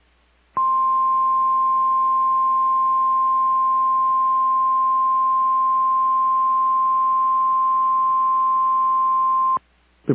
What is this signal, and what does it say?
Emergency Alert System (SAME) tones (the 1050 Hz attention tone). It is heard from 0.5 to 9.6 s.
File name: EAS Logs / KMNR